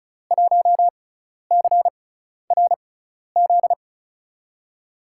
Morse code, heard as a keyed tone, about 35 wpm, spaced as for 12 wpm, 700 Hz. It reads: 1CRZ
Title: USA Call Signs - 5 Characters - 6x Character Spacing - 35wpm